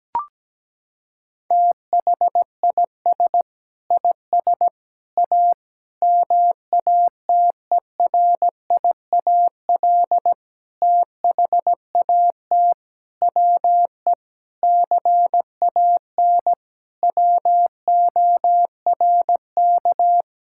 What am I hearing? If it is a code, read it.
Morse code, heard as a keyed tone, about 17 wpm, 700 Hz. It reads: THIS IS A MATERIAL THAT WE CAN WORK
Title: Sentences from Top 1000 Words - Review Part 5 - 17wpm